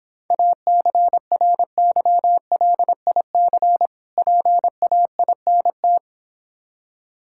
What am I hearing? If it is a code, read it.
Morse code, heard as a keyed tone, about 26 wpm, 700 Hz. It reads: ACRYLIC PAINT